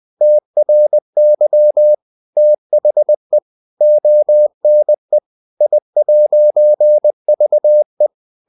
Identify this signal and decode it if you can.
Morse code, heard as a keyed tone, about 20 wpm, 600 Hz. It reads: TRY THE ONE I'VE
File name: CW / Practice / EnglishQuotations0320